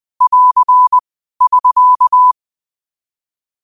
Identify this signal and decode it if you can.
Morse code, heard as a keyed tone, about 20 wpm, 1000 Hz. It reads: <AR> <SK>
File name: qotdfiles / 20wpm